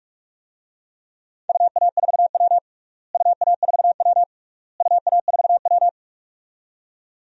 Morse code, heard as a keyed tone, about 46 wpm, 700 Hz. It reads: UA4W UA4W UA4W
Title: DXCC-100 Call Signs - 4 Characters - Repeated 3x - 45wpm